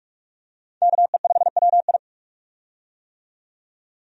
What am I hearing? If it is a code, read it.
Morse code, heard as a keyed tone, about 46 wpm, 700 Hz. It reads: KE5WI
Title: USA Call Signs - 5 Characters - 45wpm